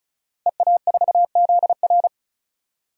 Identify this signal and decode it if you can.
Morse code, heard as a keyed tone, about 36 wpm, 700 Hz. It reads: EA4ZR
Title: DXCC-100 Call Signs - 5 Characters - 35wpm